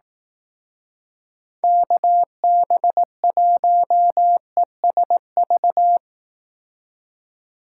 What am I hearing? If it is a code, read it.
Morse code, heard as a keyed tone, about 18 wpm, 700 Hz. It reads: KB1ESV
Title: Speed Racing - USA 2x3 Call Signs 15wpm